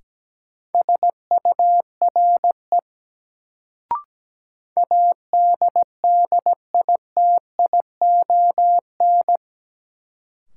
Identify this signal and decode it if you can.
Morse code, heard as a keyed tone, about 17 wpm, 700 Hz. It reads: SURE ADDITION
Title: Sets of 3 Words - New Word Drill - 17wpm